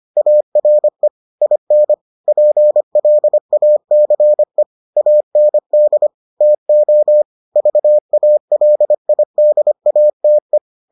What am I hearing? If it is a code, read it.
Morse code, heard as a keyed tone, about 25 wpm, 600 Hz. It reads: ARE IN PLACE AND TO VALIDATE